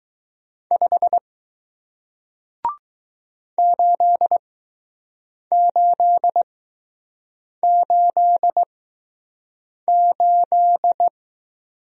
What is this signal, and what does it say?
Morse code, heard as a keyed tone, about 20 wpm, spaced as for 6 wpm, 700 Hz. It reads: 5 8888